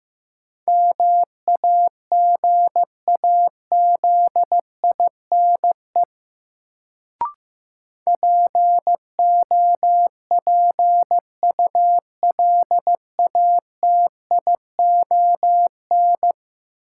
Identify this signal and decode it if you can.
Morse code, heard as a keyed tone, about 16 wpm, 700 Hz. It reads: MAGAZINE POPULATION
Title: Sentences from Top 1000 Words - Review Part 7 - New Word Drill - 15wpm